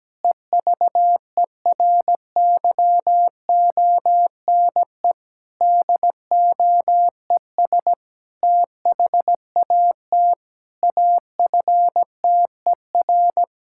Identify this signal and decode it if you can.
Morse code, heard as a keyed tone, about 17 wpm, 700 Hz. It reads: EVERYONE DOES THAT AFTER
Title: Sentences from Top 1000 Words - Review Part 5 - 17wpm